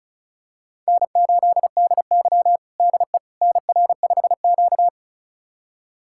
Morse code, heard as a keyed tone, about 36 wpm, 700 Hz. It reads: N8DY DE NR5Q
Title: USA Callsign DE Callsign - 4 Characters - 35wpm